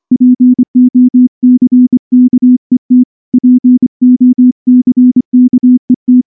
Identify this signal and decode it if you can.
Morse code, heard as a keyed tone, about 24 wpm, 250 Hz. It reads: POCKET POCKET
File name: morse / morse-training-25WPM